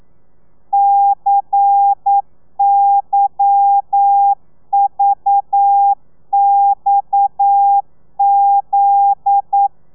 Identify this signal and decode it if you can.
Morse code, heard as a keyed tone, about 9 wpm, 800 Hz. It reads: CYVXZ